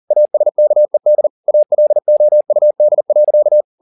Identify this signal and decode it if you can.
Morse code, heard as a keyed tone, about 40 wpm, 600 Hz. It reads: ASKED ALOUD.
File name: the old man and the sea 2 40wpm